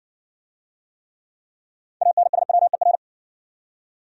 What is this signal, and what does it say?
Morse code, heard as a keyed tone, about 60 wpm, 700 Hz. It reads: ANSWER